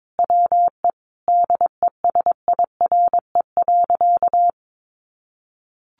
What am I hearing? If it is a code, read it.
Morse code, heard as a keyed tone, about 22 wpm, 700 Hz. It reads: WE DESIRE.
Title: Sentences from Top 1300 Words 22wpm